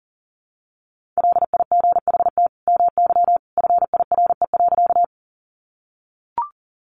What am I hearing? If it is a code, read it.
Morse code, heard as a keyed tone, about 40 wpm, 700 Hz. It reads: LIGHT MY FIRE.
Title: Sentences from Top 400 Words 40wpm